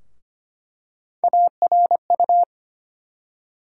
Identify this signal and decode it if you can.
Morse code, heard as a keyed tone, about 25 wpm, 700 Hz. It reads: ARU
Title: USA Call Sign Postfixes - 3 Letters 25wpm